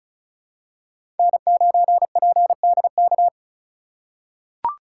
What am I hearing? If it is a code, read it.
Morse code, heard as a keyed tone, about 34 wpm, 700 Hz. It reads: N9PDK
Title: USA Call Signs - 5 Characters - 2x Character Spacing - 35wpm